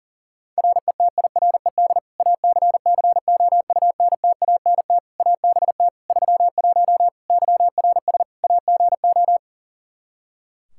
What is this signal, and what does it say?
Morse code, heard as a keyed tone, about 40 wpm, 700 Hz. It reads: RETIRED ACCOUNTANT ABT 31 YRS AGO
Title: K1USN SST QSO Practice 40wpm